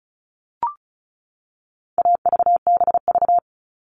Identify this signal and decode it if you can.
Morse code, heard as a keyed tone, about 34 wpm, 700 Hz. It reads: AVBV
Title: All Cut Numbers - 4-Digit 35wpm